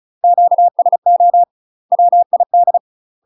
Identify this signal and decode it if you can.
Morse code, heard as a keyed tone, about 34 wpm, 700 Hz. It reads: QSO WID
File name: UK MASTER QSOs TRAINING / 35WPM / UK MASTER QSOs 0021